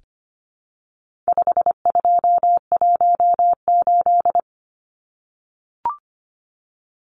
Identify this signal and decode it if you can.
Morse code, heard as a keyed tone, about 25 wpm, 700 Hz. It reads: 5218
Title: Numbers - 4-Digit 25wpm